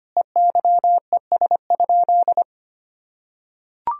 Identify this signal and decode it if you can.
Morse code, heard as a keyed tone, about 25 wpm, 700 Hz. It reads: EYES?
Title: Sentences from Top 1000 Words - Review Part 4 - 25wpm